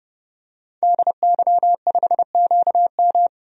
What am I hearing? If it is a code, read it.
Morse code, heard as a keyed tone, about 31 wpm, 700 Hz. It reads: DY5QM
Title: DXCC-100 Call Signs - 5 Characters - 30wpm